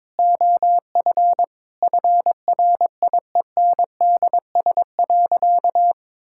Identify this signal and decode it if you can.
Morse code, heard as a keyed tone, about 22 wpm, 700 Hz. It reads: OF FRIENDS.
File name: practice / 22wpm / Sentences with 4-Word Phrases - 22wpm